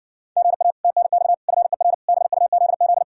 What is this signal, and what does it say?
Morse code, heard as a keyed tone, about 60 wpm, 700 Hz. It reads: GA TNX FER BUZZ